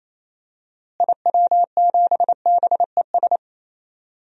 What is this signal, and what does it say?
Morse code, heard as a keyed tone, about 28 wpm, 700 Hz. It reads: IW7BES
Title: DXCC-100 Call Signs - 6 Characters - Repeated 3x - 28wpm